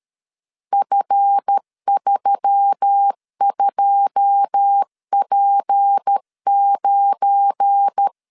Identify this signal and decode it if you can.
Morse code, heard as a keyed tone, about 13 wpm, 800 Hz. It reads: F32P9
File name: LEZIONI / 10 SIGNS / 10A SLASH 12wpm 2314